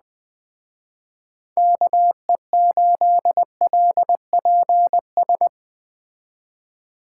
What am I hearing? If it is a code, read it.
Morse code, heard as a keyed tone, about 20 wpm, 700 Hz. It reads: KE8LPS
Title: Speed Racing - USA Call Signs 17wpm